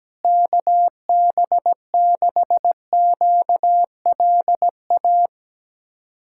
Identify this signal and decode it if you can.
Morse code, heard as a keyed tone, about 17 wpm, 700 Hz. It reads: KB6QLA
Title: USA Callsign DE Callsign 17wpm